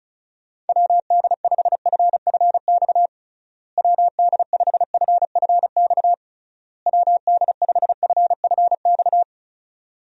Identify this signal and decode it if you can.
Morse code, heard as a keyed tone, about 36 wpm, 700 Hz. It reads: WD5FFX WD5FFX WD5FFX
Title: USA Call Signs - 6 Characters - Repeated 3x - 35wpm